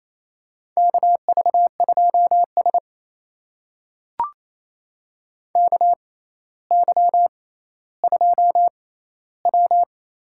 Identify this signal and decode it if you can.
Morse code, heard as a keyed tone, about 28 wpm, spaced as for 14 wpm, 700 Hz. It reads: KV2S KY2W
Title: USA Call Signs - 4 Characters - 6x Character Spacing - 28wpm